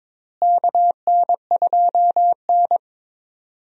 Morse code, heard as a keyed tone, about 22 wpm, 700 Hz. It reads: KN2N
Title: USA Call Signs - 4 Characters - Repeated 3x - 22wpm